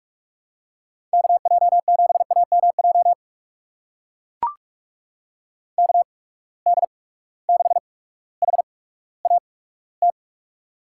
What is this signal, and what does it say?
Morse code, heard as a keyed tone, about 44 wpm, spaced as for 19 wpm, 700 Hz. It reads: KJ7AMJ KD6HAT
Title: USA Call Signs - 6 Characters - 8x Character Spacing - 45wpm